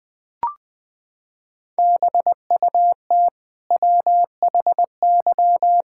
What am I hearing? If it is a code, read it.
Morse code, heard as a keyed tone, about 20 wpm, 700 Hz. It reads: BUT WHY
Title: Sentences from Top 1000 Words - Review Part 6 - 20wpm